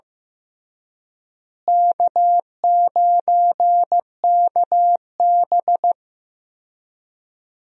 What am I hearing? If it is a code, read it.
Morse code, heard as a keyed tone, about 15 wpm, 700 Hz. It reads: K9KB
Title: USA Call Signs - 4 Characters - Repeated 3x - 15wpm